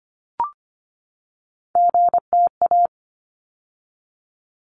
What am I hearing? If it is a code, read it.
Morse code, heard as a keyed tone, about 25 wpm, 700 Hz. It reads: GTA
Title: ARRL Sections 25wpm